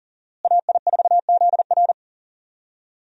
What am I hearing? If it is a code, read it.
Morse code, heard as a keyed tone, about 38 wpm, 700 Hz. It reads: AI4ZR
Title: USA Call Signs - 5 Characters - Repeated 3x - 40wpm